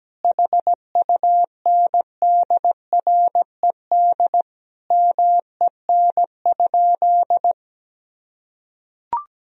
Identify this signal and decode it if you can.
Morse code, heard as a keyed tone, about 17 wpm, 700 Hz. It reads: HUNDRED MEN?
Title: Sentences from Top 1000 Words - Review Part 2 - 17wpm